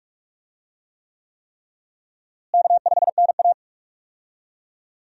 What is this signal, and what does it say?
Morse code, heard as a keyed tone, about 46 wpm, 700 Hz. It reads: K5NA
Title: USA Call Signs - 4 Characters - Repeated 3x - 45wpm